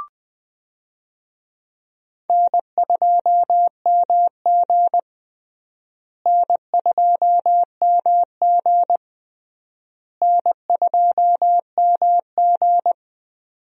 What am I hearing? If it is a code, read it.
Morse code, heard as a keyed tone, about 20 wpm, 700 Hz. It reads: N2MG N2MG N2MG
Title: USA Call Signs - 4 Characters - Repeated 3x - 20wpm